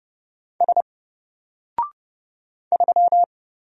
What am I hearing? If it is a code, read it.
Morse code, heard as a keyed tone, about 29 wpm, 700 Hz. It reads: S 3
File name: Single-Letter-Number-30wpm